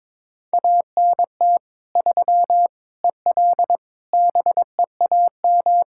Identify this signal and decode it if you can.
Morse code, heard as a keyed tone, about 22 wpm, 700 Hz. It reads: ANT 3 EL BEAM
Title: K1USN SST QSO Practice 22wpm